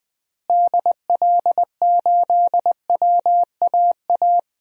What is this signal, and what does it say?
Morse code, heard as a keyed tone, about 20 wpm, 700 Hz. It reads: DL8WAA
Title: DXCC-100 Call Signs - 6 Characters - Repeated 2x - 20wpm